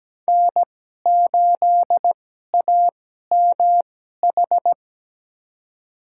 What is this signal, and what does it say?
Morse code, heard as a keyed tone, about 17 wpm, spaced as for 12 wpm, 700 Hz. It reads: N8AMH
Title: USA Call Signs - 5 Characters - 2x Character Spacing - 17wpm